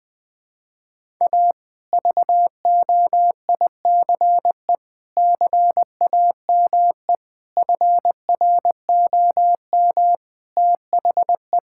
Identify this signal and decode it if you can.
Morse code, heard as a keyed tone, about 20 wpm, 700 Hz. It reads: A VOICE CAME FROM THE
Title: Sentences from Top 1000 Words - Review Part 5 - 20wpm